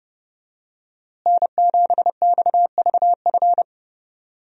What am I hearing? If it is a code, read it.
Morse code, heard as a keyed tone, about 30 wpm, 700 Hz. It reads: N7XVF
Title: USA Call Signs - 5 Characters - Repeated 2x - 30wpm